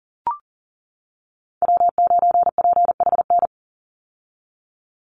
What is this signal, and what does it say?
Morse code, heard as a keyed tone, about 40 wpm, 700 Hz. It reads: W9PHN
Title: USA Call Signs 40wpm